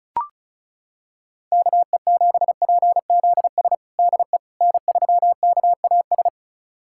Morse code, heard as a keyed tone, about 34 wpm, 700 Hz. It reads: KE7PZS DE N3KAS
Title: USA Callsign DE Callsign 35wpm